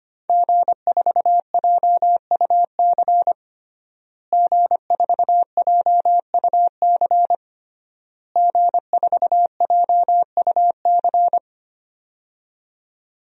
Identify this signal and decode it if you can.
Morse code, heard as a keyed tone, about 25 wpm, 700 Hz. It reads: G4JUC G4JUC G4JUC
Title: DXCC-100 Call Signs - 5 Characters - Repeated 3x - 25wpm